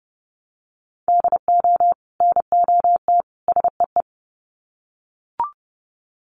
Morse code, heard as a keyed tone, about 30 wpm, 700 Hz. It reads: DO NOT SEE
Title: Sets of 3 Words 30wpm